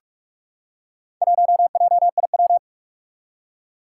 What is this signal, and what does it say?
Morse code, heard as a keyed tone, about 46 wpm, 700 Hz. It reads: 1JIW
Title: Four Challenging Letter-Number Combinations 45wpm